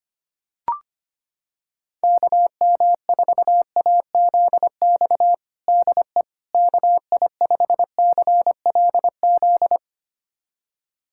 Morse code, heard as a keyed tone, about 25 wpm, 700 Hz. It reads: KM4AZX DE KI5CLZ
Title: USA Callsign DE Callsign - 6 Characters - 25wpm